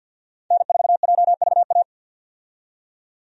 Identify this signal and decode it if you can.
Morse code, heard as a keyed tone, about 48 wpm, 700 Hz. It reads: N4JVA
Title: USA Call Signs - 5 Characters - 50wpm